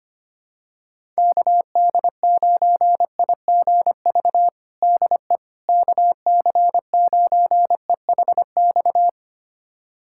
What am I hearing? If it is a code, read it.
Morse code, heard as a keyed tone, about 25 wpm, 700 Hz. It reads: KD9IGV DE KC9EHX
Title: USA Callsign DE Callsign - 6 Characters - 25wpm